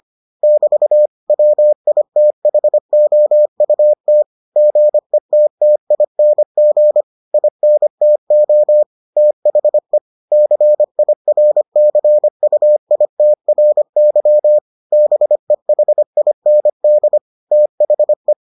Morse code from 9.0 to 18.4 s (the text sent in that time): THE CIRCUITRY BEHIND THE